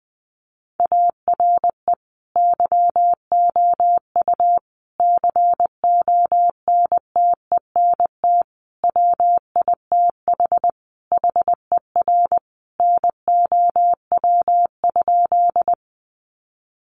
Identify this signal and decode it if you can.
Morse code, heard as a keyed tone, about 20 wpm, 700 Hz. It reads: ARE YOU CONTENT WITH HER NOW?
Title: Sentences from Top 900 Words 20wpm